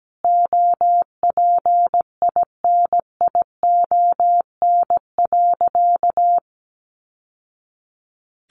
Morse code, heard as a keyed tone, about 17 wpm, 700 Hz. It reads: OPINION.
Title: Sentences from Top 1200 Words 17wpm